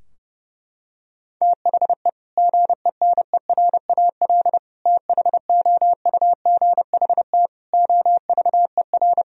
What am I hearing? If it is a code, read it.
Morse code, heard as a keyed tone, about 32 wpm, 700 Hz. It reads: THE GENERAL THOUGHT OVER
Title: Sentences from Top 1000 Words - Review Part 3 - 30wpm